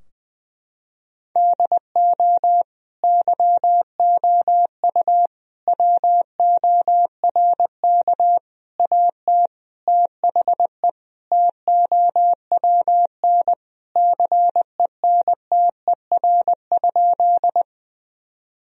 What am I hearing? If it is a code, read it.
Morse code, heard as a keyed tone, about 20 wpm, 700 Hz. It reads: DO YOU WORK AT THE TOWN CENTER?
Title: Sentences from Top 1000 Words - Review Part 2 - 20wpm